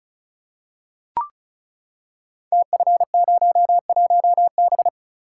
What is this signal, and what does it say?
Morse code, heard as a keyed tone, about 36 wpm, 690 Hz. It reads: TF01B
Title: Code Groups - 5 Letter:Numbers - 35wpm